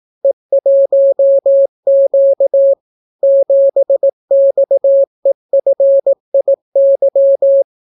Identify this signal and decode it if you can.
Morse code, heard as a keyed tone, about 18 wpm, 550 Hz. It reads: E1Q 7XEFIY